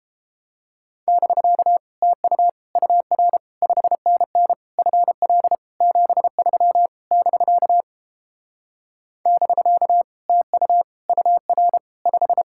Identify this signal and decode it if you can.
Morse code, heard as a keyed tone, about 33 wpm, 700 Hz. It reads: <BK> TU UR 5NN FL 73 <BK> <BK> TU UR 5